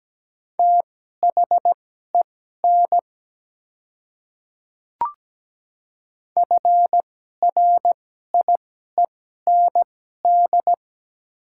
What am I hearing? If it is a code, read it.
Morse code, heard as a keyed tone, about 17 wpm, spaced as for 12 wpm, 700 Hz. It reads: THEN FRIEND
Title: 2x Character Spacing - Top 300 Words - 17wpm